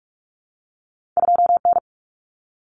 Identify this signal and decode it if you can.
Morse code, heard as a keyed tone, about 46 wpm, 700 Hz. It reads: 2N
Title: Common Cut Numbers - 2-Digit 45wpm